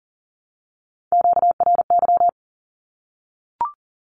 Morse code, heard as a keyed tone, about 40 wpm, 700 Hz. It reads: QRY
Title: Q-Codes 40wpm